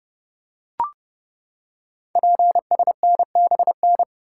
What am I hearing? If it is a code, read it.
Morse code, heard as a keyed tone, about 30 wpm, 700 Hz. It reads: PSNBN
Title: Code Groups - 5 Letters - 30wpm